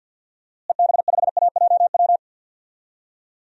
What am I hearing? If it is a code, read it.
Morse code, heard as a keyed tone, about 52 wpm, 700 Hz. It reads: EB5AJW
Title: DXCC-100 Call Signs - 6 Characters - Repeated 3x - 50wpm